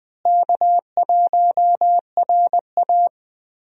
Morse code, heard as a keyed tone, about 20 wpm, 700 Hz. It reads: K1RA
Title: USA Call Signs - 4 Characters - Repeated 2x - 20wpm